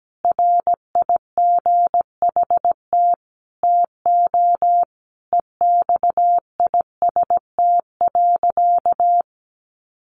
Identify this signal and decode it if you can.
Morse code, heard as a keyed tone, about 17 wpm, 700 Hz. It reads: RIGHT TO EXIST.
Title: Sentences from Top 1400 Words 17wpm